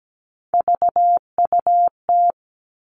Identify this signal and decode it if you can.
Morse code, heard as a keyed tone, about 17 wpm, 700 Hz. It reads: VUT
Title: All Cut Numbers - 3-Digit 17wpm